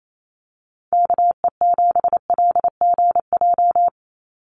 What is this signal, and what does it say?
Morse code, heard as a keyed tone, about 28 wpm, 700 Hz. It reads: KE7LGJ
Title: USA Call Signs 28wpm